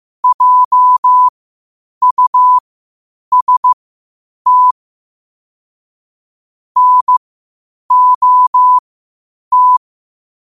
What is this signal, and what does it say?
Morse code, heard as a keyed tone, about 15 wpm, spaced as for 8 wpm, 1000 Hz. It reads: JUST NOT
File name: qotdfiles / 8wpm